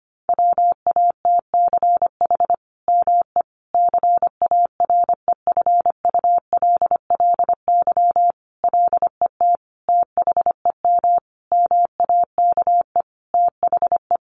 Morse code, heard as a keyed tone, about 25 wpm, 700 Hz. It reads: WATCH ME CAREFULLY LET THEM MAKE THE
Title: Sentences from Top 1300 Words 25wpm